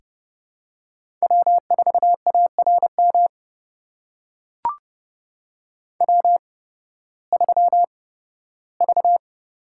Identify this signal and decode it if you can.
Morse code, heard as a keyed tone, about 32 wpm, spaced as for 16 wpm, 700 Hz. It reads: W4ARM W3V
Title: USA Call Signs - 5 Characters - 8x Character Spacing - 30wpm